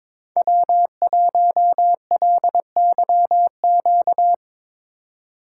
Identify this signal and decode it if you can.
Morse code, heard as a keyed tone, about 22 wpm, 700 Hz. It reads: W1LYQ
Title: USA Call Signs - 5 Characters - 2x Character Spacing - 22wpm